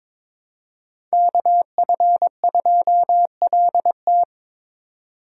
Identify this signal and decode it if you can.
Morse code, heard as a keyed tone, about 25 wpm, 700 Hz. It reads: KF2LT
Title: Speed Racing - USA 2x2 Call Signs 22wpm